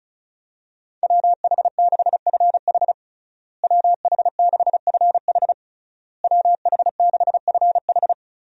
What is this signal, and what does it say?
Morse code, heard as a keyed tone, about 34 wpm, 700 Hz. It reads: WH6FH WH6FH WH6FH
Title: USA Call Signs - 5 Characters - Repeated 3x - 35wpm